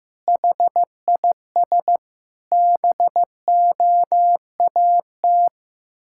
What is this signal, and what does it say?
Morse code, heard as a keyed tone, about 16 wpm, 700 Hz. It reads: HIS BOAT
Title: Pangram Challenge v2 - 15wpm